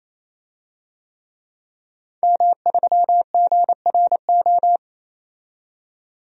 Morse code, heard as a keyed tone, about 28 wpm, 700 Hz. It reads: M3GRO
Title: DXCC-100 Call Signs - 5 Characters - 28wpm